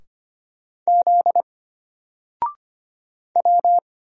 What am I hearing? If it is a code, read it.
Morse code, heard as a keyed tone, about 25 wpm, 700 Hz. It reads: Z W
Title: Single Letter-Number Alt 25wpm